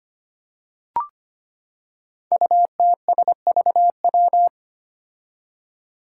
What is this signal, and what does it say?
Morse code, heard as a keyed tone, about 25 wpm, 700 Hz. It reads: UTSVW
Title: Five Challenging Letter-Number Combinations 25wpm